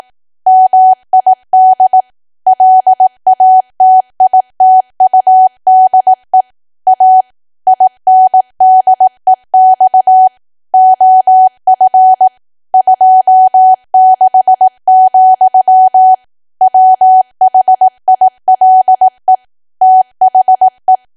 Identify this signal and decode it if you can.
Morse code, heard as a keyed tone, about 18 wpm, 750 Hz. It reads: MID LATITUDE A INDEX OF 26, WHILE THE
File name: cwa / W1AW-Level2 / 18WPM